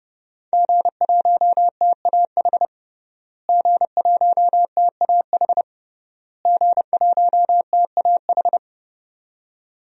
Morse code, heard as a keyed tone, about 32 wpm, 700 Hz. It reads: G1TAH G1TAH G1TAH
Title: DXCC-100 Call Signs - 5 Characters - Repeated 3x - 30wpm